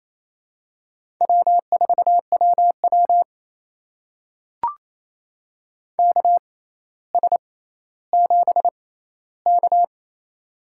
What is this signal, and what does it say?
Morse code, heard as a keyed tone, about 28 wpm, spaced as for 14 wpm, 700 Hz. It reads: W4WW KS7K
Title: USA Call Signs - 4 Characters - 6x Character Spacing - 28wpm